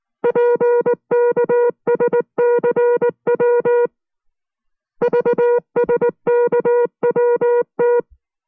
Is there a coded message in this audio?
Yes: Morse code from 0.2 to 8.0 s, heard as a keyed tone, about 19 wpm, 480 Hz. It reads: PKSCW VSKWT